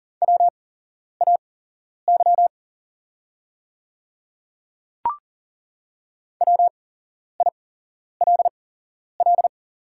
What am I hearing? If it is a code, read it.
Morse code, heard as a keyed tone, about 40 wpm, spaced as for 11 wpm, 700 Hz. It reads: WAY WILL
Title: 8x Character Spacing - Top 100 Words - 40wpm